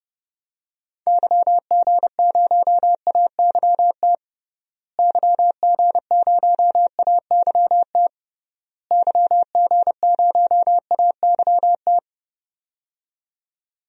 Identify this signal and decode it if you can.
Morse code, heard as a keyed tone, about 30 wpm, 700 Hz. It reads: YG0AYT YG0AYT YG0AYT
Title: DXCC-100 Call Signs - 6 Characters - Repeated 3x - 30wpm